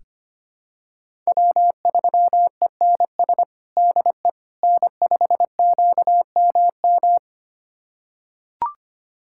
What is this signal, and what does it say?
Morse code, heard as a keyed tone, about 25 wpm, 700 Hz. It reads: W3ENS DE N5QMM
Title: USA Callsign DE Callsign - 5 Characters - 25wpm